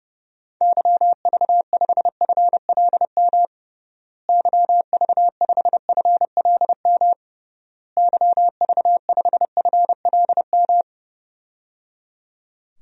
Morse code, heard as a keyed tone, about 32 wpm, 700 Hz. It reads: YV5FLM YV5FLM YV5FLM
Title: DXCC-100 Call Signs - 6 Characters - Repeated 3x - 30wpm